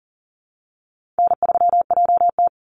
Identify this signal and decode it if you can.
Morse code, heard as a keyed tone, about 40 wpm, 700 Hz. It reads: N3JT
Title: Easy Call Signs - 40wpm